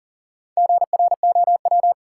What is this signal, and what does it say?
Morse code, heard as a keyed tone, about 40 wpm, 700 Hz. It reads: GROW
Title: Sentences from Top 300 Words - New Word Drill - 40wpm